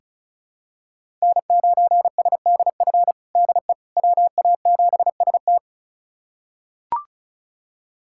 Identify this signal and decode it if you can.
Morse code, heard as a keyed tone, about 36 wpm, 690 Hz. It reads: N9SDF DE WA7ST
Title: USA Callsign DE Callsign - 5 Characters - 35wpm